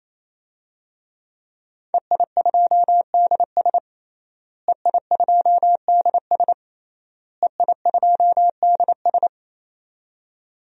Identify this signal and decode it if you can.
Morse code, heard as a keyed tone, about 27 wpm, 700 Hz. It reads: EI2DS EI2DS EI2DS
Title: DXCC-100 Call Signs - 5 Characters - Repeated 3x - 28wpm